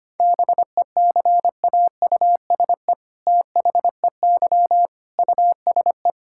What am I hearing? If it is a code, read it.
Morse code, heard as a keyed tone, about 25 wpm, 700 Hz. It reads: BECAUSE THEY USE
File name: practice / 25wpm / All-Age Appropriate Jokes - 25wpm